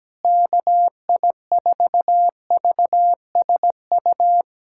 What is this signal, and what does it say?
Morse code, heard as a keyed tone, about 17 wpm, 700 Hz. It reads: KI4VSU
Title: USA Callsign DE Callsign - 6 Characters - 17wpm